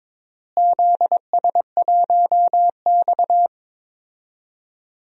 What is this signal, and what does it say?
Morse code, heard as a keyed tone, about 22 wpm, 700 Hz. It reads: ZS1X
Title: DXCC-100 Call Signs - 4 Characters - Repeated 2x - 22wpm